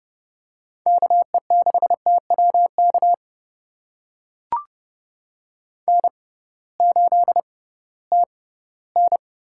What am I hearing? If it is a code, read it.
Morse code, heard as a keyed tone, about 32 wpm, spaced as for 11 wpm, 700 Hz. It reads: KE6TW KN8TN
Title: USA Call Signs - 6x Character Spacing - 30wpm